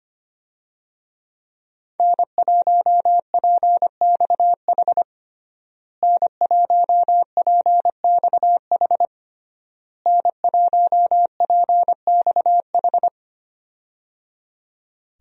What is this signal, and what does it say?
Morse code, heard as a keyed tone, about 25 wpm, 700 Hz. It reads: N1PXH N1PXH N1PXH
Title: USA Call Signs - 5 Characters - Repeated 3x - 25wpm